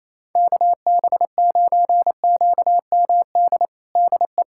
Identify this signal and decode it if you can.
Morse code, heard as a keyed tone, about 28 wpm, 700 Hz. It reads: KB9QMD DE
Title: USA Callsign DE Callsign 28wpm